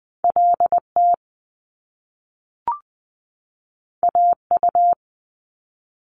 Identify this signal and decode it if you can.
Morse code, heard as a keyed tone, about 20 wpm, 700 Hz. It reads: LT AU
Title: Top Two Letter Combinations in Words 20wpm